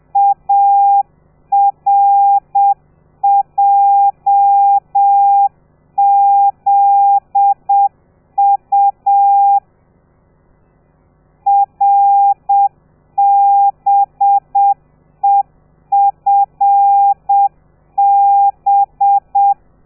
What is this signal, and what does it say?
Morse code, heard as a keyed tone, about 7 wpm, 800 Hz. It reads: ARJZU RBEFB
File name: telegrafia / i2tfj / Lez13 30